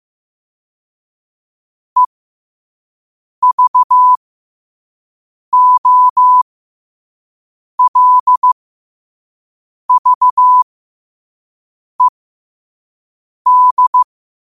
Morse code, heard as a keyed tone, about 15 wpm, spaced as for 5 wpm, 1000 Hz. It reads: EVOLVED
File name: qotdfiles / 5wpm